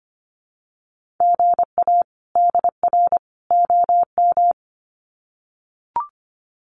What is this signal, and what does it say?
Morse code, heard as a keyed tone, about 24 wpm, 700 Hz. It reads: GA DR OM
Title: Common QSO Phrases 25wpm